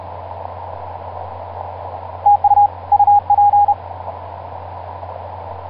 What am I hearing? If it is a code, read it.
Morse code, heard as a keyed tone, about 31 wpm, 820 Hz. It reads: TU UP
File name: pj2-oh1vr iz2amw etc 12m cw 24mar2010 1955